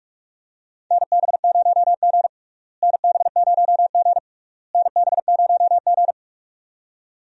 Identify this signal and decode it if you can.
Morse code, heard as a keyed tone, about 44 wpm, 700 Hz. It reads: NB0G NB0G NB0G
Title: USA Call Signs - 4 Characters - Repeated 3x - 45wpm